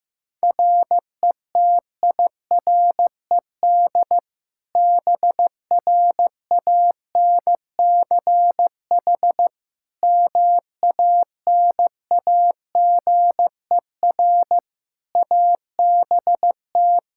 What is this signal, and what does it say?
Morse code, heard as a keyed tone, about 16 wpm, 700 Hz. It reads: RETIRED BRANCH MANAGER ABT
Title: Practice QSO - Exchange 3 - More Info - 15wpm